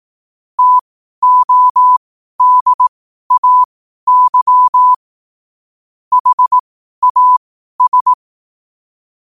Morse code, heard as a keyed tone, about 18 wpm, spaced as for 12 wpm, 1000 Hz. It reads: TODAY HAS
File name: qotdfiles / 12wpm